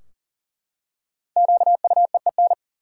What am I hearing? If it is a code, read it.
Morse code, heard as a keyed tone, about 40 wpm, 700 Hz. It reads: QUEEN
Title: Five Letter Words - 40wpm